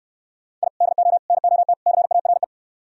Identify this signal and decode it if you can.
Morse code, heard as a keyed tone, about 65 wpm, 700 Hz. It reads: I DO NOT CARE